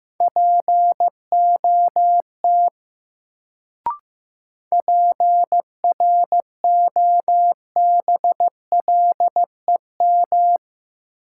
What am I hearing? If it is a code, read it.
Morse code, heard as a keyed tone, about 15 wpm, 700 Hz. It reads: POT PROBLEM
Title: Sentences from Top 1000 Words - New Word Drill - 15wpm